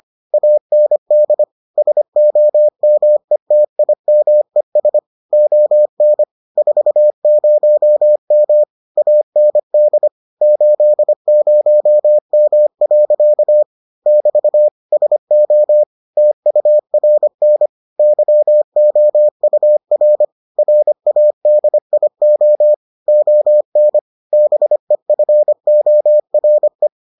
Morse code, heard as a keyed tone, about 25 wpm, 600 Hz. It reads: AND SOMETIMES ON 40M AND 80M. <BT> SO TURN YOUR RADIO ON BEFORE